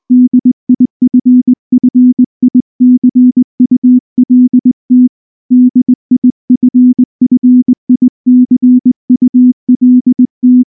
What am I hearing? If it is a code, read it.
Morse code, heard as a keyed tone, about 19 wpm, 250 Hz. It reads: DIFFICULT DIFFICULT
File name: morse / morse-training-20WPM-6